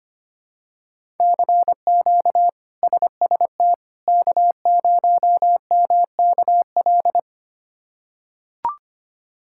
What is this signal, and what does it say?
Morse code, heard as a keyed tone, about 25 wpm, 700 Hz. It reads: CQ SST K0MKL
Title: K1USN SST QSO Practice - 25wpm